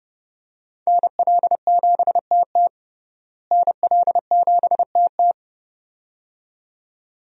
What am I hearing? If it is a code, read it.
Morse code, heard as a keyed tone, about 32 wpm, 700 Hz. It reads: NL7TT NL7TT
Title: USA Call Signs - 5 Characters - Repeated 2x - 30wpm